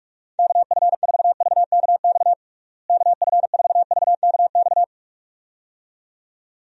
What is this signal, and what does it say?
Morse code, heard as a keyed tone, about 46 wpm, 700 Hz. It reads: KF4VKX KF4VKX
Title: USA Call Signs - 6 Characters - Repeated 2x - 45wpm